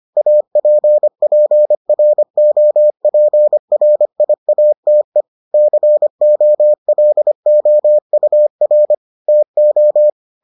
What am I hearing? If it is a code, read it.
Morse code, heard as a keyed tone, about 25 wpm, 600 Hz. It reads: APPROPRIATE COLOUR TO